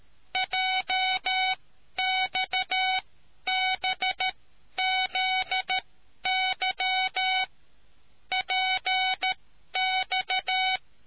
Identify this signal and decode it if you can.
Morse code, heard as a keyed tone, about 13 wpm, spaced as for 10 wpm, 750 Hz. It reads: JXBZY PX